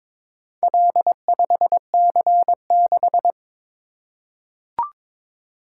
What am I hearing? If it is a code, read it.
Morse code, heard as a keyed tone, about 22 wpm, 700 Hz. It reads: L5C6
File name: practice / 22wpm / Four Letter-Number Combinations - 22wpm